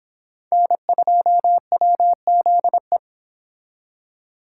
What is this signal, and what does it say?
Morse code, heard as a keyed tone, about 31 wpm, 700 Hz. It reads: N2WZE